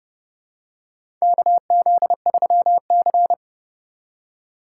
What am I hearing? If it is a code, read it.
Morse code, heard as a keyed tone, about 30 wpm, 700 Hz. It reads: KZ3C
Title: USA Call Signs - 4 Characters - 3x Character Spacing - 30wpm